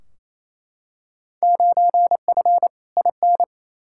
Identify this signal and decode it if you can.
Morse code, heard as a keyed tone, about 28 wpm, 700 Hz. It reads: 9F IN
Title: Field Day Exchanges Encore - 28wpm